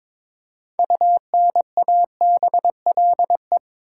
Morse code, heard as a keyed tone, about 22 wpm, 700 Hz. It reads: UNABLE
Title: Six Letter Words - 22wpm